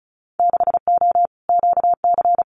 Morse code, heard as a keyed tone, about 36 wpm, 700 Hz. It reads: 6O QC
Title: Winter Field Day Exchanges Encore 35wpm